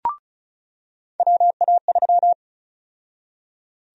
Morse code, heard as a keyed tone, about 34 wpm, 700 Hz. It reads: WA3
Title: USA Call Sign Prefixes - 35wpm